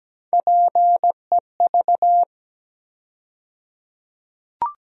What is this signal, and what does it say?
Morse code, heard as a keyed tone, about 17 wpm, 700 Hz. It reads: PEV
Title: USA Call Sign Postfixes - 3 Letters 17wpm